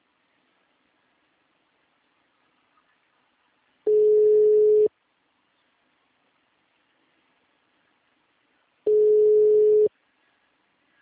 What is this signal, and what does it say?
Ringback tone (Europe, 425 Hz).